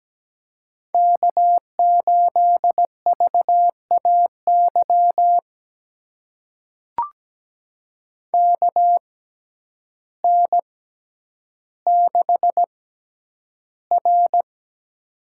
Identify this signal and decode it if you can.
Morse code, heard as a keyed tone, about 17 wpm, spaced as for 9 wpm, 700 Hz. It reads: K8VAY KN6R
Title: USA Call Signs - 6x Character Spacing - 17wpm